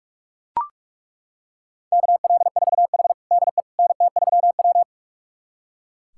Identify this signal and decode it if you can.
Morse code, heard as a keyed tone, about 42 wpm, 700 Hz. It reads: KL4H DE NT3W
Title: USA Callsign DE Callsign - 4 Characters - 45wpm